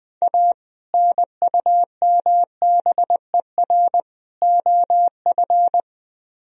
Morse code, heard as a keyed tone, about 20 wpm, 700 Hz. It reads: A NUMBER OF